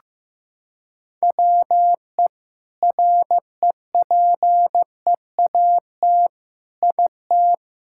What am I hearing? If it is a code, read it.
Morse code, heard as a keyed tone, about 16 wpm, 700 Hz. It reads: WE REPEAT IT
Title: Sentences from Top 1600 Words 15wpm